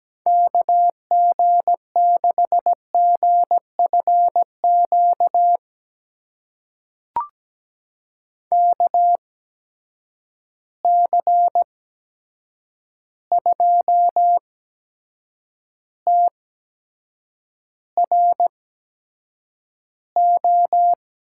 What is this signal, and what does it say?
Morse code, heard as a keyed tone, about 17 wpm, spaced as for 7 wpm, 700 Hz. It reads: KG6GFQ KC2TRO
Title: USA Call Signs - 6 Characters - 8x Character Spacing - 17wpm